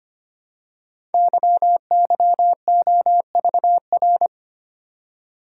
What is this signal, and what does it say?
Morse code, heard as a keyed tone, about 25 wpm, 700 Hz. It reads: YYOVR
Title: Code Groups - 5 Letters - 25wpm